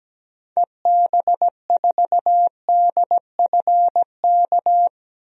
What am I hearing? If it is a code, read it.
Morse code, heard as a keyed tone, about 17 wpm, 700 Hz. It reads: EB4DFK
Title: DXCC-100 Call Signs - 6 Characters - 17wpm